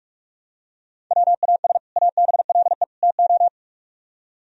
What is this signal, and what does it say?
Morse code, heard as a keyed tone, about 47 wpm, 700 Hz. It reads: WAS ABLE TO